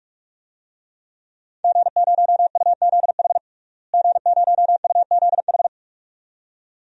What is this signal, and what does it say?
Morse code, heard as a keyed tone, about 42 wpm, 700 Hz. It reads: G0UZH G0UZH
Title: DXCC-100 Call Signs - 5 Characters - Repeated 2x - 45wpm